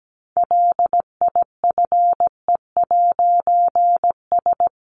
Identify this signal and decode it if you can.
Morse code, heard as a keyed tone, about 17 wpm, 700 Hz. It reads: LIFE'S
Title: A Sound Of Thunder 17wpm